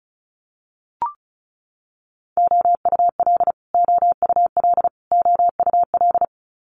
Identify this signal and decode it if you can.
Morse code, heard as a keyed tone, about 36 wpm, 700 Hz. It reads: OUL OUL OUL
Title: Multi-letter Phonemes Repeated 3x 35wpm